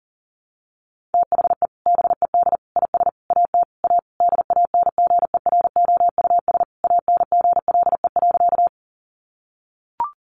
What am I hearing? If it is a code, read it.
Morse code, heard as a keyed tone, about 40 wpm, 700 Hz. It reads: THE BED IS AT A DANGEROUS ANGLE.
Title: Sentences from Top 1500 Words 40wpm